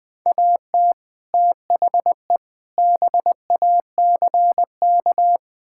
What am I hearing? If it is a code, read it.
Morse code, heard as a keyed tone, about 20 wpm, 700 Hz. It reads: AT THE BACK
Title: Build A Phrase - 5-Word Phrases - 20wpm